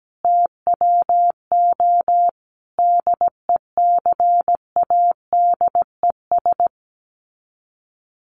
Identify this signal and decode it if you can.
Morse code, heard as a keyed tone, about 17 wpm, 700 Hz. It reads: TWO DECADES
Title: Sets of 5 Words Encore2 17wpm